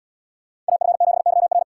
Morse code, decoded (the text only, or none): AF7YR